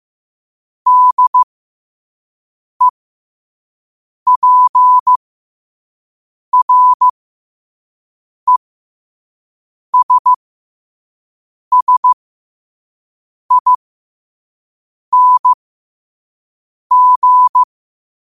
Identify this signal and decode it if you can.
Morse code, heard as a keyed tone, about 15 wpm, spaced as for 5 wpm, 1000 Hz. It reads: DEPRESSING
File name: qotdfiles / 5wpm